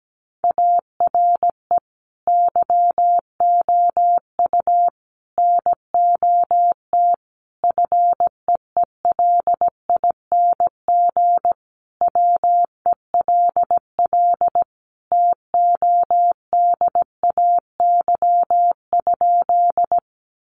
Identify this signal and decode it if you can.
Morse code, heard as a keyed tone, about 17 wpm, 700 Hz. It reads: ARE YOU NOT FEELING WELL TODAY?
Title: Sentences from Top 1200 Words 17wpm